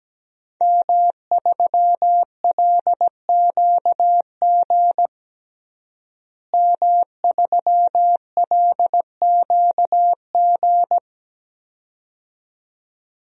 Morse code, heard as a keyed tone, about 17 wpm, 700 Hz. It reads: M3LQG M3LQG
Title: DXCC-100 Call Signs - 5 Characters - Repeated 2x - 17wpm